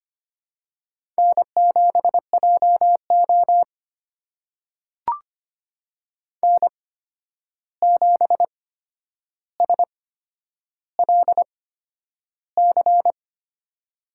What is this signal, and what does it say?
Morse code, heard as a keyed tone, about 25 wpm, spaced as for 10 wpm, 700 Hz. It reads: N7JO N7SLC
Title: USA Call Signs - 8x Character Spacing - 25wpm